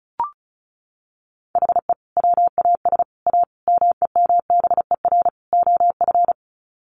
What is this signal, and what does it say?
Morse code, heard as a keyed tone, about 34 wpm, 700 Hz. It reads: HE WAS A MEMBER OF
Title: Sets of 5 Words Encore2 35wpm